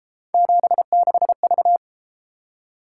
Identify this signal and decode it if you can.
Morse code, heard as a keyed tone, about 33 wpm, 700 Hz. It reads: 76V